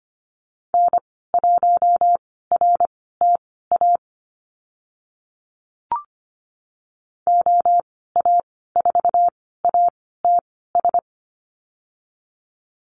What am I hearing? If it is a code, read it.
Morse code, heard as a keyed tone, about 25 wpm, spaced as for 16 wpm, 700 Hz. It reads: N1RTA OA4ATS
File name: Lesson 15g - Number 5 - Call Signs 25wpm